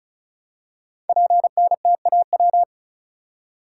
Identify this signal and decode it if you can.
Morse code, heard as a keyed tone, about 36 wpm, 700 Hz. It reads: PNTAW
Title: Code Groups - 5 Letters - 35wpm